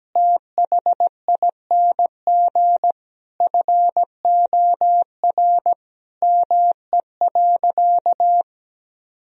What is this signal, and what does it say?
Morse code, heard as a keyed tone, about 17 wpm, 700 Hz. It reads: THING FOR ME.
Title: Sentences from Top 1000 Words - Review Part 4 - 17wpm